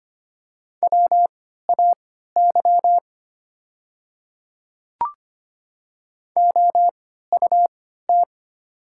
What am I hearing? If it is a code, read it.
Morse code, heard as a keyed tone, about 25 wpm, spaced as for 14 wpm, 700 Hz. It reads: WAY OUT
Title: 3x Character Spacing - Top 200 Words - 25wpm